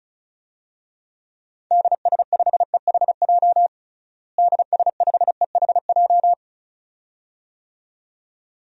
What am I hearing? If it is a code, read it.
Morse code, heard as a keyed tone, about 34 wpm, 700 Hz. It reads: DS5EHJ DS5EHJ
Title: DXCC-100 Call Signs - 6 Characters - Repeated 2x - 35wpm